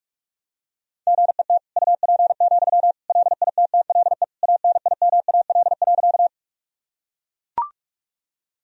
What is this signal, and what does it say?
Morse code, heard as a keyed tone, about 41 wpm, 700 Hz. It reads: GET UP, LITTLE ANIMAL.
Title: Sentences from Top 1000 Words - Review Part 1 - 45wpm